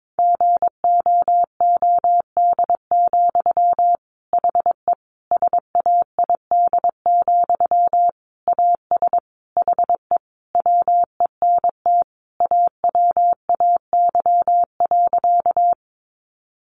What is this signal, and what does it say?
Morse code, heard as a keyed tone, about 22 wpm, 700 Hz. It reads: GOOD, HE SAID, AS HE WENT AWAY.
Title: Sentences from Top 600 Words 22wpm